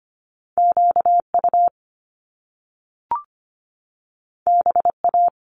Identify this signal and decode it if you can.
Morse code, heard as a keyed tone, about 25 wpm, 700 Hz. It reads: QU BA
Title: Top Two Letter Combinations in Words 25wpm